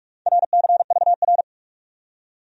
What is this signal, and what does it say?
Morse code, heard as a keyed tone, about 56 wpm, 700 Hz. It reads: RCVR